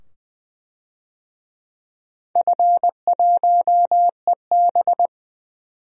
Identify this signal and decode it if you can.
Morse code, heard as a keyed tone, about 20 wpm, 700 Hz. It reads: F1EB
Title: DXCC-100 Call Signs - 4 Characters - 20wpm